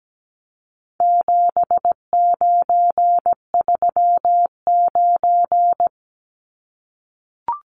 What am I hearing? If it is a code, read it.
Morse code, heard as a keyed tone, about 17 wpm, 700 Hz. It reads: 7939
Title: Numbers - 4-Digit 17wpm